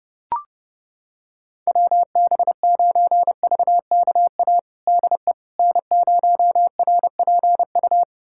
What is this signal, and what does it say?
Morse code, heard as a keyed tone, about 30 wpm, 700 Hz. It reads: WB9VKA DE N0RPU
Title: USA Callsign DE Callsign 30wpm